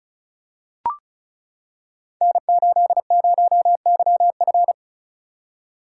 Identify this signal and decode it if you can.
Morse code, heard as a keyed tone, about 36 wpm, 700 Hz. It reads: N80YF
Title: Code Groups - 5 Letter:Numbers - 35wpm